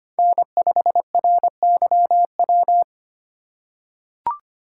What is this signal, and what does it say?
Morse code, heard as a keyed tone, about 25 wpm, 700 Hz. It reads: N5RYW
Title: USA Call Signs - 5 Characters - 3x Character Spacing - 25wpm